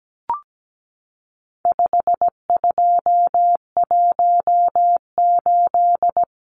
Morse code, heard as a keyed tone, about 17 wpm, 700 Hz. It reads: 5218
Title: Numbers - 4-Digit 17wpm